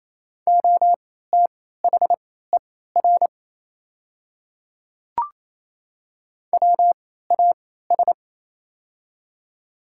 Morse code, heard as a keyed tone, about 27 wpm, spaced as for 16 wpm, 700 Hz. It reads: OTHER WAS
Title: 3x Character Spacing - Top 100 Words - 28wpm